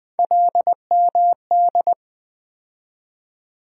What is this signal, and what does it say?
Morse code, heard as a keyed tone, about 20 wpm, 700 Hz. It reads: LMD
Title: USA Call Sign Postfixes - 3 Letters 20wpm